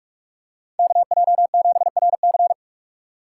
Morse code, heard as a keyed tone, about 44 wpm, 700 Hz. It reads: KJ7RC